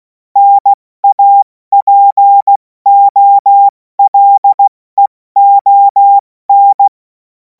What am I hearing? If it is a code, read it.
Morse code, heard as a keyed tone, about 16 wpm, 800 Hz. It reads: NAPOLEON
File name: Morsekurs / Witze / Witz 26. Mai 2026 70 F80